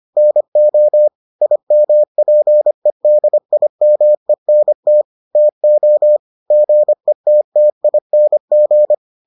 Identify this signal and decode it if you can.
Morse code, heard as a keyed tone, about 25 wpm, 600 Hz. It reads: NO IMPEDIMENT TO GETTING